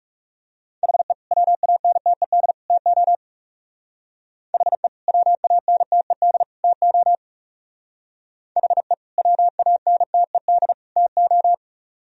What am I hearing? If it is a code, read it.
Morse code, heard as a keyed tone, about 38 wpm, 700 Hz. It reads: HE WANTED TO HE WANTED TO HE WANTED TO